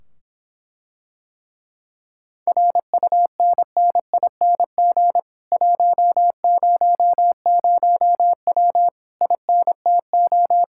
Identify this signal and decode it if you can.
Morse code, heard as a keyed tone, about 26 wpm, 700 Hz. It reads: RUNNING 100W INTO